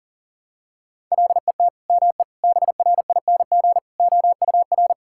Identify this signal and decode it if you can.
Morse code, heard as a keyed tone, about 40 wpm, 700 Hz. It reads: LET ME BRING OUR
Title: Sentences from Top 1000 Words - Review Part 6 - 40wpm